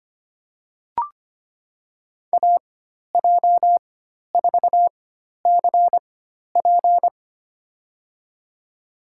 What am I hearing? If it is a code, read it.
Morse code, heard as a keyed tone, about 25 wpm, spaced as for 12 wpm, 700 Hz. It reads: AJ4CP
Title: USA Call Signs - 5 Characters - 4x Character Spacing - 25wpm